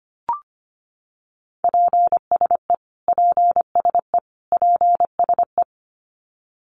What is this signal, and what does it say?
Morse code, heard as a keyed tone, about 25 wpm, 700 Hz. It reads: PSE PSE PSE
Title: Abbreviations Repeated 3x - 25wpm